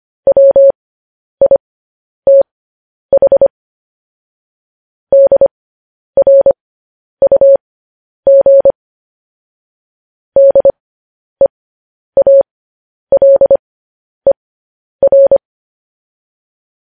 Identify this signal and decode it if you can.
Morse code, heard as a keyed tone, about 25 wpm, spaced as for 10 wpm, 550 Hz. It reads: WITH DRUG DEALER
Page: news-headlines-in-morse-code-at-10-wpm